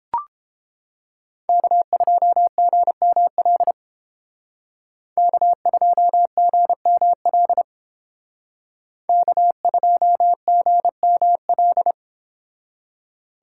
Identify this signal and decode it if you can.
Morse code, heard as a keyed tone, about 29 wpm, 700 Hz. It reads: K2GML K2GML K2GML